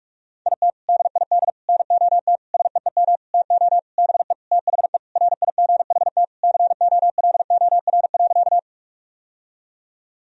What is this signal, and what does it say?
Morse code, heard as a keyed tone, about 46 wpm, 700 Hz. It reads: IT DID NOT SEEM TO BE THE RIGHT COLOR.
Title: Sentences from Top 1000 Words - Review Part 2 - 45wpm